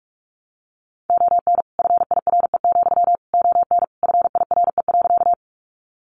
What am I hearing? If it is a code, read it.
Morse code, heard as a keyed tone, about 46 wpm, 700 Hz. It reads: ON FIRE, ON FIRE.
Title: Sentences from Top 400 Words 45wpm